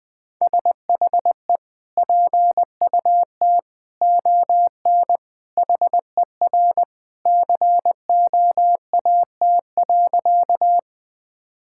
Morse code, heard as a keyed tone, about 21 wpm, 700 Hz. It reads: SHE PUT ON HER COAT.
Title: Sentences from Top 1600 Words 20wpm